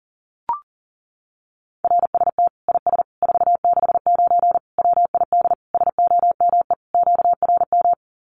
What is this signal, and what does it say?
Morse code, heard as a keyed tone, about 40 wpm, 700 Hz. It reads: RST IS 469 WID SOME QRM
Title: Common QSO Phrases 40wpm